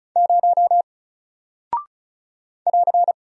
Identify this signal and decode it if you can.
Morse code, heard as a keyed tone, about 36 wpm, 700 Hz. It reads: 0 <AR>
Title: Single Character (Letter:Number:Punctuation:Prosign) Alt 35wpm